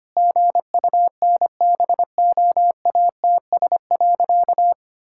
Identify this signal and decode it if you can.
Morse code, heard as a keyed tone, about 25 wpm, 700 Hz. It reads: GUNBOATS.
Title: Pangram Challenge v2 - 25wpm